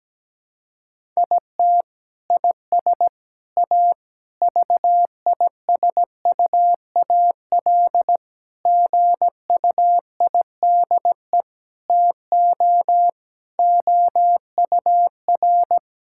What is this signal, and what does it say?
Morse code, heard as a keyed tone, about 17 wpm, 700 Hz. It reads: IT IS A VISUAL GUIDE TO OUR
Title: Sentences from Top 1000 Words - Review Part 6 - 17wpm